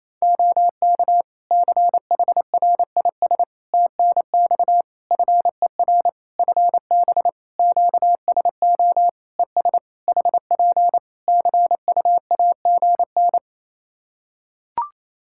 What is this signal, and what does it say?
Morse code, heard as a keyed tone, about 28 wpm, 700 Hz. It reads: OK CHRIS TNX FER FB QSO ES HP CUAGN
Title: Practice QSO - Exchange 4 - Ending - 28wpm